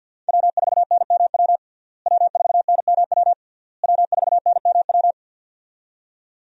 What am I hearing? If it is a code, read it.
Morse code, heard as a keyed tone, about 49 wpm, 700 Hz. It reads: W4NMW W4NMW W4NMW
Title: USA Call Signs - 5 Characters - Repeated 3x - 50wpm